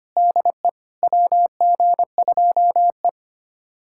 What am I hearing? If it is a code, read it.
Morse code, heard as a keyed tone, about 25 wpm, 700 Hz. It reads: DE WG2E
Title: USA Callsign DE Callsign - 4 Characters - 25wpm